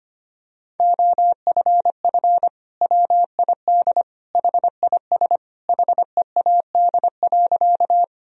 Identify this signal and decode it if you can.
Morse code, heard as a keyed tone, about 25 wpm, 700 Hz. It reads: OFF WID HIS HEAD.
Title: Sentences from Top 300 Words and Abbreviations 25wpm